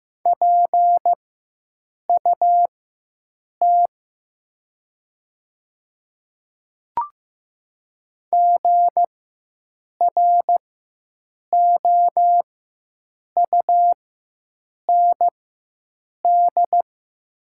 Morse code, heard as a keyed tone, about 15 wpm, spaced as for 7 wpm, 700 Hz. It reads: PUT GROUND
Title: 4x Character Spacing - Top 500 Words - 15wpm